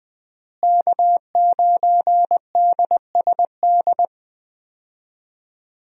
Morse code, heard as a keyed tone, about 21 wpm, 700 Hz. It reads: K9DSD
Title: USA Call Signs - 5 Characters - Repeated 3x - 20wpm